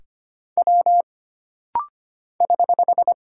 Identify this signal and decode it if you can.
Morse code, heard as a keyed tone, about 25 wpm, 700 Hz. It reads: W <HH>
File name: Single Character (Letter:Number:Punctuation:Prosign) - Rapid-Fire - 25wpm